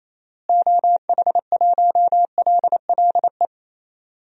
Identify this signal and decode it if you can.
Morse code, heard as a keyed tone, about 28 wpm, 700 Hz. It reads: OH1LLE
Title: DXCC-100 Call Signs - 6 Characters - Repeated 2x - 28wpm